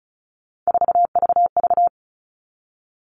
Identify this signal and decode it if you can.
Morse code, heard as a keyed tone, about 34 wpm, 700 Hz. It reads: 4VV
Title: All Cut Numbers - 3-Digit 35wpm